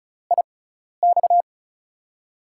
Morse code, heard as a keyed tone, about 36 wpm, spaced as for 12 wpm, 700 Hz. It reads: IX
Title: USA Call Signs - 4 Characters - 6x Character Spacing - 35wpm